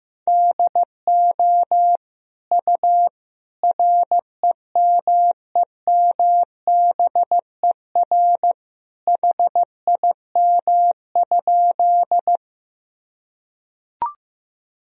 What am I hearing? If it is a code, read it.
Morse code, heard as a keyed tone, about 15 wpm, 700 Hz. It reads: DO U REMEMBER HIM?
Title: Sentences from Top 400 Words and Abbreviations 15wpm